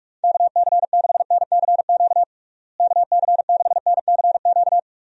Morse code, heard as a keyed tone, about 43 wpm, 700 Hz. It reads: KC6NCQ KC6NCQ
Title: USA Call Signs - 6 Characters - Repeated 2x - 45wpm